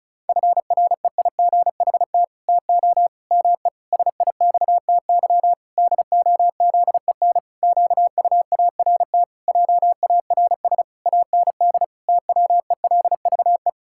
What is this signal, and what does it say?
Morse code, heard as a keyed tone, about 34 wpm, 700 Hz. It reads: FREIGHT TO ME SIXTY DOZEN QUART JARS AND TWELVE
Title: Pangram Challenge v2 - 35wpm